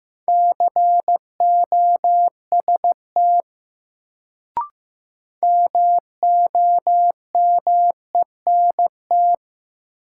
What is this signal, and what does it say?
Morse code, heard as a keyed tone, about 15 wpm, 700 Hz. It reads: COST MOMENT
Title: Sets of 3 Words - New Word Drill - 15wpm